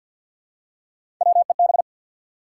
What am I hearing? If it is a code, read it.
Morse code, heard as a keyed tone, about 53 wpm, 700 Hz. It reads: WEB
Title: Three Letter Words - 50wpm